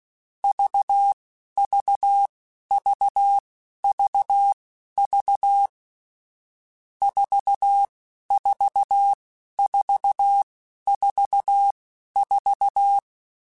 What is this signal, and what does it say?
Morse code, heard as a keyed tone, about 16 wpm, spaced as for 11 wpm, 800 Hz. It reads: VVVVV 44444